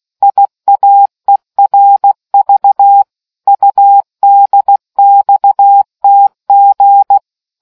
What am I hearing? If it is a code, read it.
Morse code, heard as a keyed tone, about 16 wpm, 800 Hz. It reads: IAERV UDXTG